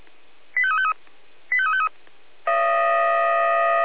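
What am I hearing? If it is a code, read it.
Five-tone selective call (ZVEI), heard as quick runs of five tones, tones of about 70 ms. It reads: 85353 85353
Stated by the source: Five-tone ZVEI selective call received by SDR.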